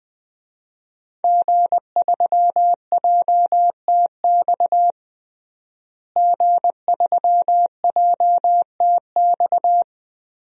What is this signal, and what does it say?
Morse code, heard as a keyed tone, about 20 wpm, 700 Hz. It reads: G3JTX G3JTX
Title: DXCC-100 Call Signs - 5 Characters - Repeated 3x - 20wpm